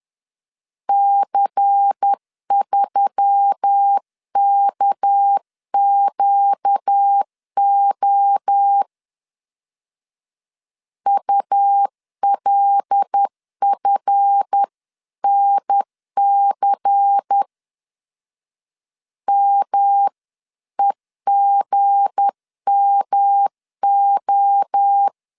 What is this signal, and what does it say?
Morse code, heard as a keyed tone, about 10 wpm, 800 Hz. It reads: C3KQO ULFNC MEGMO
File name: LEZIONI / 7 GF9Z / 7 GF 10wpm 2514